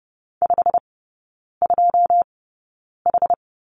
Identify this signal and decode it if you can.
Morse code, heard as a keyed tone, about 30 wpm, spaced as for 9 wpm, 700 Hz. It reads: 52H
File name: Lesson 17e - Letter u - Three Characters 30wpm